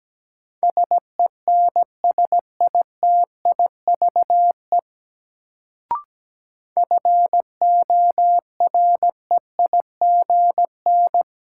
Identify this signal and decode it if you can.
Morse code, heard as a keyed tone, about 17 wpm, 700 Hz. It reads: SENSITIVE FOREIGN
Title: Sentences from Top 1400 Words - New Word Drill - 17wpm